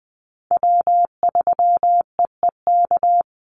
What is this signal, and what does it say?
Morse code, heard as a keyed tone, about 20 wpm, 700 Hz. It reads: W3EEK
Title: Call Signs - 20wpm